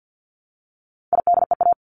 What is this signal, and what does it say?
Morse code, heard as a keyed tone, about 50 wpm, 700 Hz. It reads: IDEA
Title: Top 500 Words Repeated 50wpm